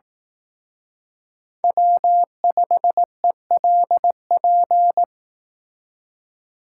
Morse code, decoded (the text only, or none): W5ELP